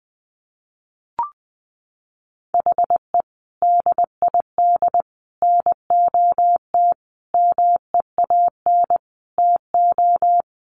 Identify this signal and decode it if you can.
Morse code, heard as a keyed tone, about 20 wpm, 700 Hz. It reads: HE DID NOT MEAN TO
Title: Sentences from Top 1500 Words 20wpm